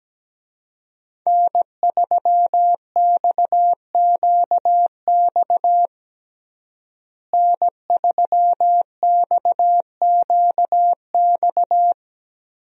Morse code, heard as a keyed tone, about 17 wpm, 700 Hz. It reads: N3XQX N3XQX
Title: USA Call Signs - 5 Characters - Repeated 2x - 17wpm